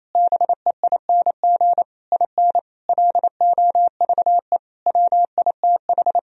Morse code, heard as a keyed tone, about 28 wpm, 700 Hz. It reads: BEING IN LOVE WITH
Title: Sentences from Top 1000 Words - Review Part 3 - 28wpm